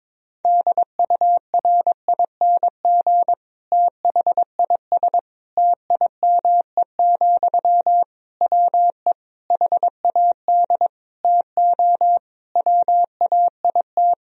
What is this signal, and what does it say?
Morse code, heard as a keyed tone, about 22 wpm, 700 Hz. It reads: DURING THIS TIME, WE HAD TO WAIT
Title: Sentences from Top 1000 Words - Review Part 2 - 22wpm